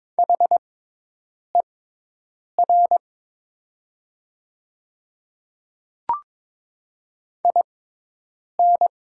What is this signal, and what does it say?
Morse code, heard as a keyed tone, about 22 wpm, spaced as for 8 wpm, 700 Hz. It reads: HER IN
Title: 6x Character Spacing - Top 100 Words - 22wpm